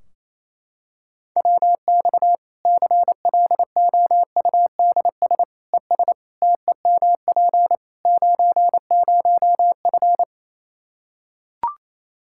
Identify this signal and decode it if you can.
Morse code, heard as a keyed tone, about 28 wpm, 700 Hz. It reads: WX CLOUDS ES TEMP 90F
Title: Practice QSO - Exchange 2 - Info - 28wpm